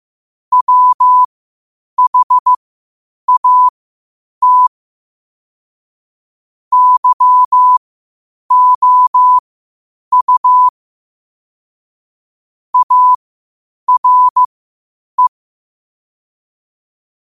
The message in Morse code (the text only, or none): WHAT YOU ARE